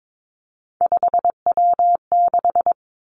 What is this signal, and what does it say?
Morse code, heard as a keyed tone, about 22 wpm, 700 Hz. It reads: 5W6
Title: Three Challenging Letter-Number Combinations 22wpm